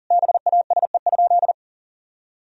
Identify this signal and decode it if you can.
Morse code, heard as a keyed tone, about 40 wpm, 700 Hz. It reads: BASE?
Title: Sentences from Top 1000 Words - Review Part 1 - 40wpm